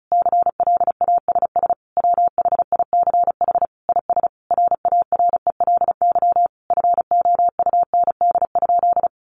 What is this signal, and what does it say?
Morse code, heard as a keyed tone, about 34 wpm, 700 Hz. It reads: CLASS WHICH IS RARELY FOUND?
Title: Sentences from Top 1400 Words 35wpm